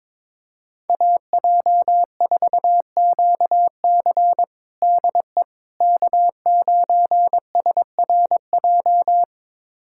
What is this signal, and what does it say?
Morse code, heard as a keyed tone, about 22 wpm, 700 Hz. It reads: AJ4QC DE K9SRJ
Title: USA Callsign DE Callsign - 5 Characters - 22wpm